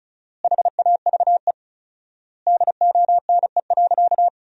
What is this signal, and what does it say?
Morse code, heard as a keyed tone, about 36 wpm, 700 Hz. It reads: HAVE DONE.
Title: 4x Word-Spacing - Sentences from Top 500 Words - 35wpm